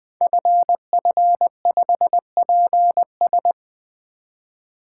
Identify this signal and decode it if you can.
Morse code, heard as a keyed tone, about 20 wpm, 700 Hz. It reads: FF5PS
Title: Code Groups - 5 Letter:Numbers - 20wpm